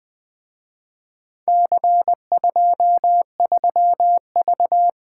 Morse code, heard as a keyed tone, about 23 wpm, 700 Hz. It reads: C23V
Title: Speed-Racing - Four Letter-Number Combinations - 20wpm